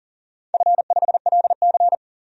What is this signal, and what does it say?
Morse code, heard as a keyed tone, about 40 wpm, 700 Hz. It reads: F5LC
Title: DXCC-100 Call Signs - 4 Characters - 40wpm